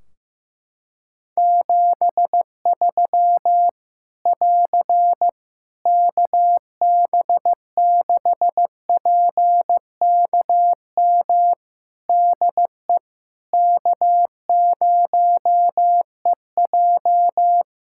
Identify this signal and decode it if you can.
Morse code, heard as a keyed tone, about 15 wpm, 700 Hz. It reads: 73 <AR> KB6PKM DE K0EJ
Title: Practice QSO - Exchange 4 - Ending - 15wpm